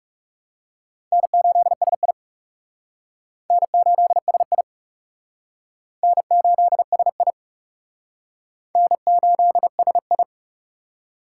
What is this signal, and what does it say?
Morse code, heard as a keyed tone, about 37 wpm, 700 Hz. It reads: N8SI N8SI N8SI N8SI
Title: Speed Racing - USA 1x2 Call Signs 30wpm